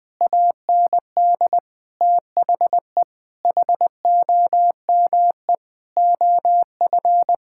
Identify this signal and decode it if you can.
Morse code, heard as a keyed tone, about 20 wpm, 700 Hz. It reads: AND THE HOME OF
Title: Sentences with 9-Word Phrases - 20wpm